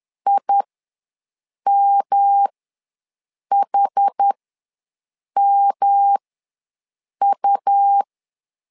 Morse code, heard as a keyed tone, about 10 wpm, spaced as for 6 wpm, 800 Hz. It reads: IMHMU
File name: LEZIONI / 3 MWRO / 3 M ws 10wpm 2121